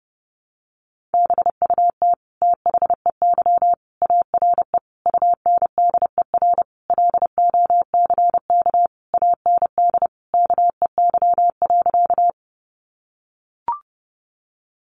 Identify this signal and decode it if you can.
Morse code, heard as a keyed tone, about 30 wpm, 700 Hz. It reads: BUT THEY ARE UNDER LOCK AND KEY.
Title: Sentences from Top 1300 Words 30wpm